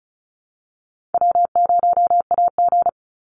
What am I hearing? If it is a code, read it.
Morse code, heard as a keyed tone, about 34 wpm, 700 Hz. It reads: W0AG
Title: Easy Call Signs Repeated 3x - 35wpm